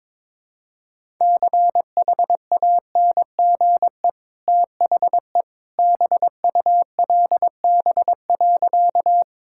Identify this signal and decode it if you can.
Morse code, heard as a keyed tone, about 22 wpm, 690 Hz. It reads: CHANGE THE BULB.
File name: practice / 22wpm / Easy Sentences with Hints - 22wpm